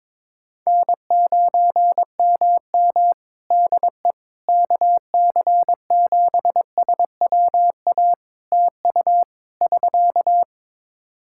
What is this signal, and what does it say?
Morse code, heard as a keyed tone, about 22 wpm, 700 Hz. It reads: N9MM DE KC7SWA TU <SK>
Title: Practice QSO - Exchange 4 - Ending - 22wpm